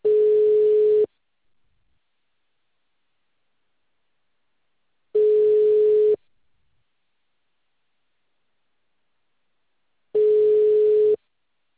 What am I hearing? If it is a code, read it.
Ringback tone (Europe, 425 Hz).
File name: out- -2044- - - .27705